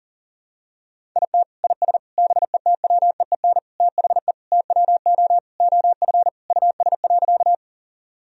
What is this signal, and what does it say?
Morse code, heard as a keyed tone, about 37 wpm, 700 Hz. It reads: IT IS BETWEEN THE TWO OF US.
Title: Sentences from Top 1000 Words - Review Part 1 - 40wpm